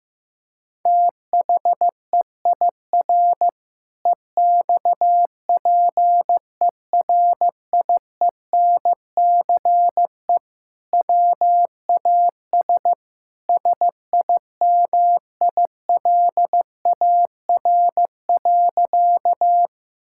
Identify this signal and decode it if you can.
Morse code, heard as a keyed tone, about 15 wpm, 700 Hz. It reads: THEIR EXPERIENCE WAS SIMILAR.
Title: Sentences from Top 1000 Words - Review Part 5 - 15wpm